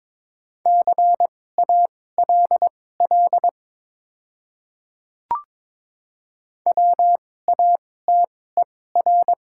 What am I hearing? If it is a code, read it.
Morse code, heard as a keyed tone, about 22 wpm, spaced as for 16 wpm, 700 Hz. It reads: CALL WATER
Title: 2x Character Spacing - Top 200 Words - 22wpm